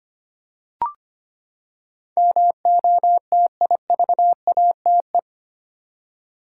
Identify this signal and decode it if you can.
Morse code, heard as a keyed tone, about 25 wpm, 700 Hz. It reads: MOTIVATE
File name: Eight Letter Words - 25wpm